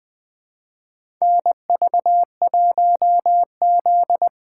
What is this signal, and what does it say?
Morse code, heard as a keyed tone, about 20 wpm, 700 Hz. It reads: NV1Z
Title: USA Call Signs - 3x Character Spacing - 20wpm